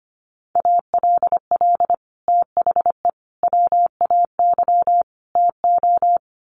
Morse code, heard as a keyed tone, about 25 wpm, 700 Hz. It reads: ALL THE WAY TO
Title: Sets of 4 Words 25wpm